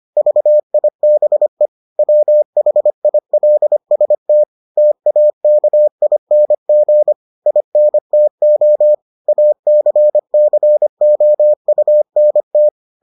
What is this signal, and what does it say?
Morse code, heard as a keyed tone, about 25 wpm, 600 Hz. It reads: VIBE WHILST TAKING INTO ACCOUNT